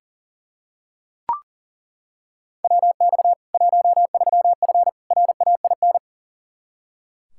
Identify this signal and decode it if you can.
Morse code, heard as a keyed tone, about 37 wpm, 700 Hz. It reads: WX 13F RAIN
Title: K1USN SST QSO Practice 40wpm